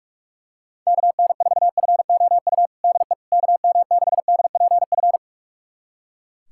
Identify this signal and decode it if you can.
Morse code, heard as a keyed tone, about 42 wpm, 700 Hz. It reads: KN4FOU DE KM6DPF
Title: USA Callsign DE Callsign - 6 Characters - 45wpm